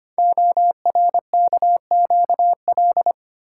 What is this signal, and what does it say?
Morse code, heard as a keyed tone, about 25 wpm, 700 Hz. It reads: ORKQL
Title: Code Groups - 5 Letters - 25wpm